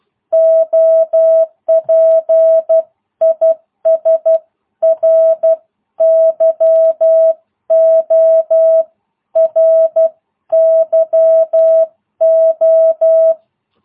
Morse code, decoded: OP IS RYO RYO

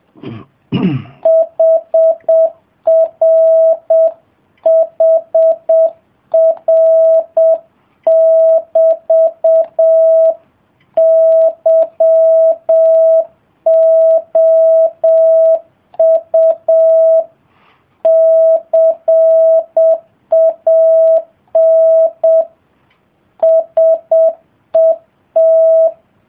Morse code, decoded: HRHR<BT> YOU CAN SET

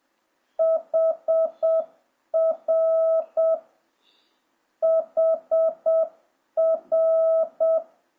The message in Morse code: HR HR

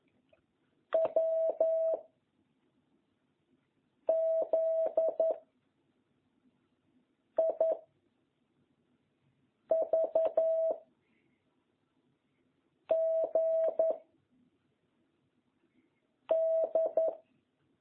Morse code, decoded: W Z I V G D

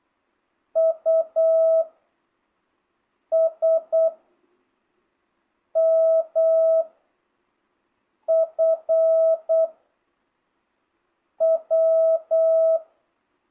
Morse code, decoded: U S M F W

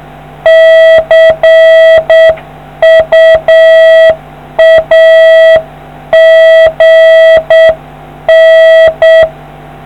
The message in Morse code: CU AGN

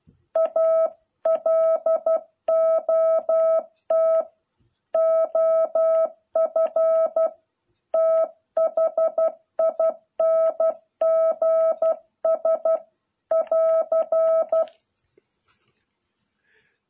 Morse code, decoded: A LOT OF THINGS <AR>

